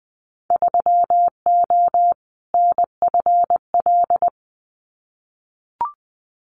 Morse code, heard as a keyed tone, about 20 wpm, 700 Hz. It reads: 3O NFL
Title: Winter Field Day Exchanges 20wpm